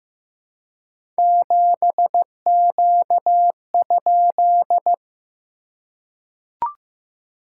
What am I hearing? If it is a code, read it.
Morse code, heard as a keyed tone, about 15 wpm, 700 Hz. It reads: 7Q?
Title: SOTA QSO Practice 15wpm